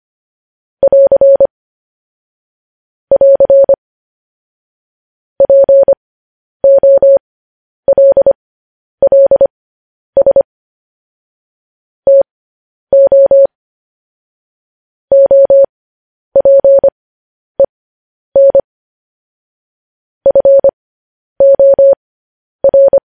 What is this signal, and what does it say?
Morse code, heard as a keyed tone, about 25 wpm, spaced as for 10 wpm, 550 Hz. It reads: <AR> <AR> POLLS TO OPEN FOR
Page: News Headlines in Morse Code at 10 WPM